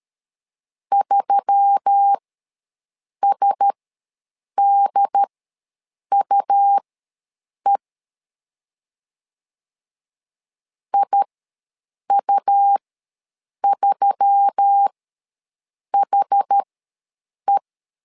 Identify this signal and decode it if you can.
Morse code, heard as a keyed tone, about 13 wpm, spaced as for 7 wpm, 800 Hz. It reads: 3SDUE IU3HE
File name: LEZIONI / 4 Q1S3 ws 12wpm 2038